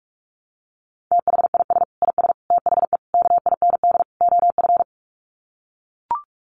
Morse code, heard as a keyed tone, about 44 wpm, 700 Hz. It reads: THIS IS THE KIND OF
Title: Sets of 5 Words Encore 45wpm